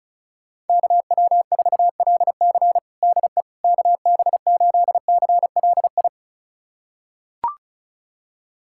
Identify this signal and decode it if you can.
Morse code, heard as a keyed tone, about 36 wpm, 700 Hz. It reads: KW4LC DE KB8CLI
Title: USA Callsign DE Callsign 35wpm